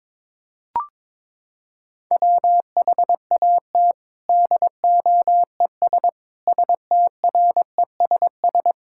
Morse code, heard as a keyed tone, about 22 wpm, 700 Hz. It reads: WHAT DOES STRESS
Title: Sentences from Top 1000 Words - Review Part 6 - 22wpm